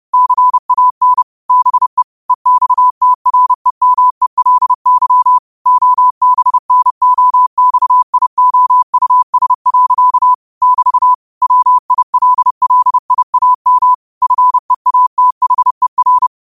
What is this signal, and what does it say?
Morse code, heard as a keyed tone, about 30 wpm, 1000 Hz. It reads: CAN BE EXTREMELY OBNOXIOUS. <BT> WILLIAM FEATHER
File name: qotdfiles / 30wpm